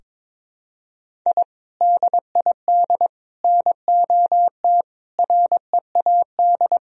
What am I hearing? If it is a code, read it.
Morse code, heard as a keyed tone, about 22 wpm, 700 Hz. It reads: I DID NOT READ
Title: Sentences from Top 1000 Words - Review Part 3 - 22wpm